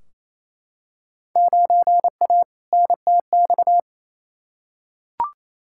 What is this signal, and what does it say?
Morse code, heard as a keyed tone, about 28 wpm, 700 Hz. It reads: 9A NTX
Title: Field Day Exchanges Encore - 28wpm